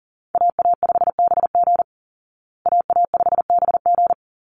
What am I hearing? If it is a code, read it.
Morse code, heard as a keyed tone, about 40 wpm, 700 Hz. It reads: AA5BG AA5BG
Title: Call Signs Repeated 3x - 40wpm